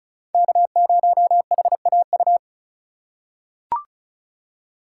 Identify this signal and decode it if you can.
Morse code, heard as a keyed tone, about 36 wpm, 700 Hz. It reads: K0HAU
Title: USA Call Signs - 5 Characters - 2x Character Spacing - 35wpm